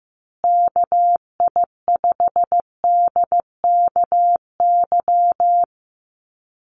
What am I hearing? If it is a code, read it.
Morse code, heard as a keyed tone, about 15 wpm, 700 Hz. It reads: KI5DKY
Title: USA Call Signs Repeated 2x 15wpm